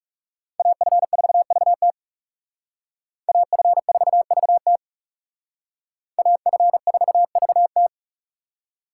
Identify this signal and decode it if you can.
Morse code, heard as a keyed tone, about 40 wpm, 700 Hz. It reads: AF4VT AF4VT AF4VT